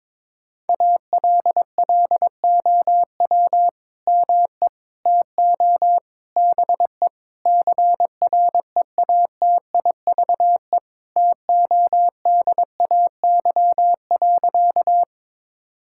Morse code, heard as a keyed tone, about 22 wpm, 700 Hz. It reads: ALLOW ME TO BE CREATIVE TODAY.
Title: Sentences from Top 1000 Words - Review Part 5 - 22wpm